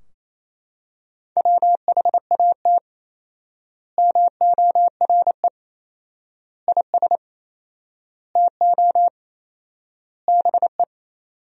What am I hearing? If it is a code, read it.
Morse code, heard as a keyed tone, about 28 wpm, 700 Hz. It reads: WHAT MORE IS TO BE
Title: 4x Word-Spacing - Sentences from Top 500 Words - 28wpm